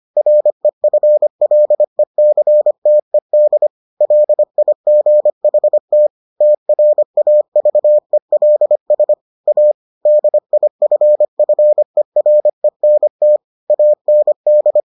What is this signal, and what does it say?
Morse code, heard as a keyed tone, about 25 wpm, 600 Hz. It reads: REFLECTED LIGHT TRAVELS A DIFFERENT AND